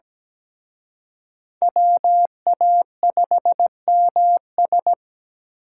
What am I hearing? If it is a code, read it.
Morse code, heard as a keyed tone, about 20 wpm, 700 Hz. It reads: WA5MS
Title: Speed Racing - USA 2x2 Call Signs 17wpm